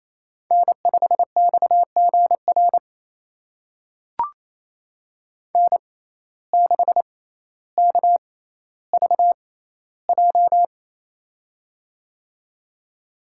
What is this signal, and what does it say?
Morse code, heard as a keyed tone, about 28 wpm, spaced as for 14 wpm, 700 Hz. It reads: N5XGR N6KVJ
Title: USA Call Signs - 5 Characters - 6x Character Spacing - 28wpm